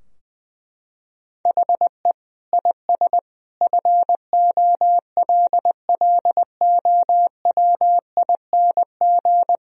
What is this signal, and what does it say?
Morse code, heard as a keyed tone, about 20 wpm, 700 Hz. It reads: HE IS FOLLOWING
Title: Sentences from Top 1000 Words - Review Part 4 - 20wpm